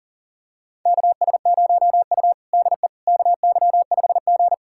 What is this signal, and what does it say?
Morse code, heard as a keyed tone, about 40 wpm, 700 Hz. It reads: KS0U DE KY5G
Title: USA Callsign DE Callsign - 4 Characters - 40wpm